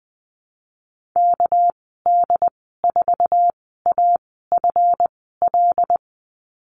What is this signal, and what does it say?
Morse code, heard as a keyed tone, about 20 wpm, spaced as for 15 wpm, 700 Hz. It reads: KD4AFL
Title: 207 Lesson 33g - Letter j - Call Signs 20wpm